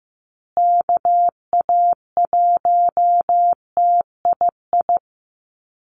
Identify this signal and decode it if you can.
Morse code, heard as a keyed tone, about 15 wpm, 700 Hz. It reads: KA1TII
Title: USA Call Signs Repeated 3x 15wpm